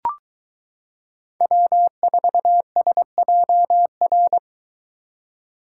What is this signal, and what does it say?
Morse code, heard as a keyed tone, about 22 wpm, 700 Hz. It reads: W4SJR